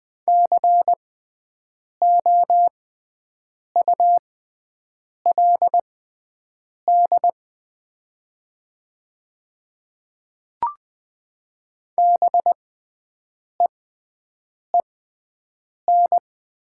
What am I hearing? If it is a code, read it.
Morse code, heard as a keyed tone, about 20 wpm, spaced as for 7 wpm, 690 Hz. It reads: COULD BEEN
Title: 6x Character Spacing - Top 100 Words - 20wpm